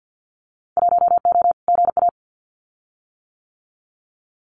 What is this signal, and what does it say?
Morse code, heard as a keyed tone, about 50 wpm, 700 Hz. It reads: 1O GA
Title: Winter Field Day Exchanges 50wpm